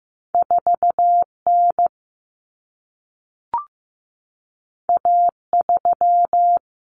Morse code, heard as a keyed tone, about 15 wpm, 700 Hz. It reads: 4 N A 3
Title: Common Cut Numbers - 2-Digit 15wpm